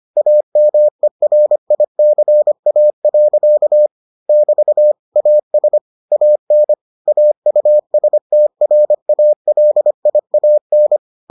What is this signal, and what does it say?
Morse code, heard as a keyed tone, about 25 wpm, 600 Hz. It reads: AMERICA. <BT> AS AN AUSTRALIAN